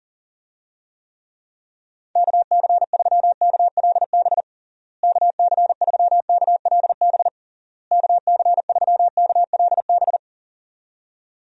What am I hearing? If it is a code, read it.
Morse code, heard as a keyed tone, about 40 wpm, 700 Hz. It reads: KC3KLB KC3KLB KC3KLB
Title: USA Call Signs - 6 Characters - Repeated 3x - 40wpm